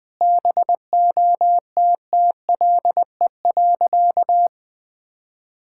Morse code, heard as a keyed tone, about 20 wpm, 700 Hz. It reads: BOTTLE.
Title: Sentences with 4-Word Phrases - 20wpm